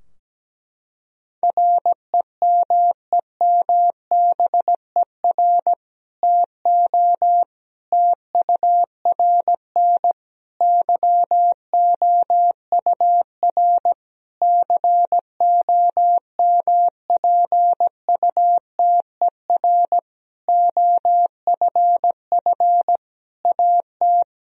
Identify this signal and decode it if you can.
Morse code, heard as a keyed tone, about 17 wpm, 700 Hz. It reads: REMEMBER TO TURN YOUR COMPUTER OFF AT
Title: Sentences from Top 1000 Words - Review Part 3 - 17wpm